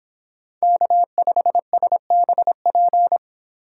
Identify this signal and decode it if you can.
Morse code, heard as a keyed tone, about 26 wpm, 700 Hz. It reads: K5SBP